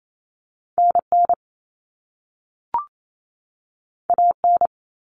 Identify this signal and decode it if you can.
Morse code, heard as a keyed tone, about 28 wpm, 700 Hz. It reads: NN AN
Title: Common Cut Numbers - 2-Digit 28wpm